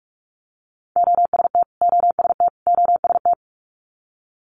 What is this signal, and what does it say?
Morse code, heard as a keyed tone, about 46 wpm, 700 Hz. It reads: OST OST OST
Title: Multi-letter Phonemes Repeated 3x 45wpm